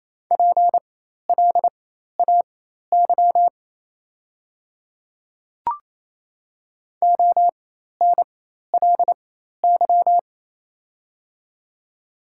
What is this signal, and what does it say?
Morse code, heard as a keyed tone, about 28 wpm, spaced as for 13 wpm, 700 Hz. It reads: PLAY ONLY
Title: 4x Character Spacing - Top 200 Words - 28wpm